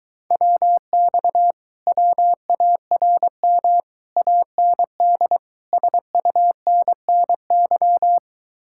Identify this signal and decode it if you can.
Morse code, heard as a keyed tone, about 23 wpm, 700 Hz. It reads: WX WARM AND SUNNY